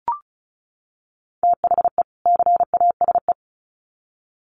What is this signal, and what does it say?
Morse code, heard as a keyed tone, about 34 wpm, 700 Hz. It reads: THE CASE
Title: Sets of 2 Words 35wpm